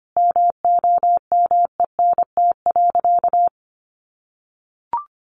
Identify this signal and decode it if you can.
Morse code, heard as a keyed tone, about 25 wpm, 700 Hz. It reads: MOMENT.
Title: Sentences from Top 1000 Words 25wpm